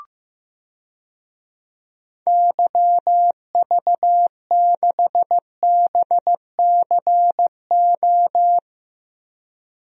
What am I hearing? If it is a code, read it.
Morse code, heard as a keyed tone, about 15 wpm, 700 Hz. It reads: YV6BCO
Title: DXCC-100 Call Signs - 6 Characters - 15wpm